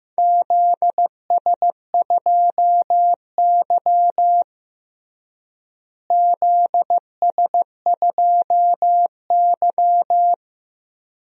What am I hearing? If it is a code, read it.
Morse code, heard as a keyed tone, about 16 wpm, 700 Hz. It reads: ZS2Y ZS2Y
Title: DXCC-100 Call Signs - 4 Characters - Repeated 2x - 15wpm